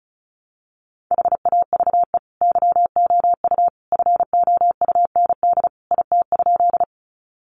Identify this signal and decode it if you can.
Morse code, heard as a keyed tone, about 34 wpm, 700 Hz. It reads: HAVE YOU FOUND IT?
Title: Sentences from Top 300 Words 35wpm